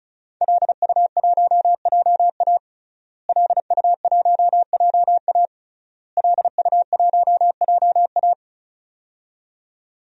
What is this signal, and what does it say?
Morse code, heard as a keyed tone, about 35 wpm, 700 Hz. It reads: LU1JA LU1JA LU1JA
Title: DXCC-100 Call Signs - 5 Characters - Repeated 3x - 35wpm